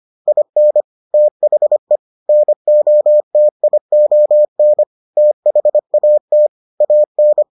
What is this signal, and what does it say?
Morse code, heard as a keyed tone, about 25 wpm, 600 Hz. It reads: IN THE NOTION THAT AN